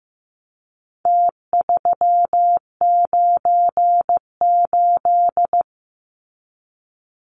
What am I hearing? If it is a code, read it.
Morse code, heard as a keyed tone, about 15 wpm, 700 Hz. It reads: T398
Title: Numbers - Contest Style - 4-Digit 15wpm